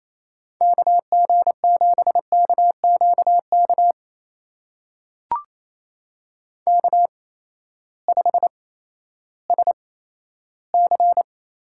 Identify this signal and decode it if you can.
Morse code, heard as a keyed tone, about 28 wpm, spaced as for 14 wpm, 700 Hz. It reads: KG7KQK K5SC
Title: USA Call Signs - 8x Character Spacing - 28wpm